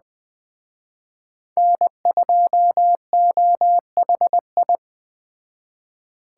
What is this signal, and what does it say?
Morse code, heard as a keyed tone, about 20 wpm, 700 Hz. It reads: N2OHI
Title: Speed Racing - USA Call Signs 17wpm